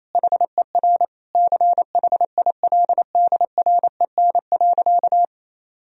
Morse code, heard as a keyed tone, about 28 wpm, 700 Hz. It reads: HER CHILDREN.
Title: Sentences from Top 1000 Words - Review Part 3 - 28wpm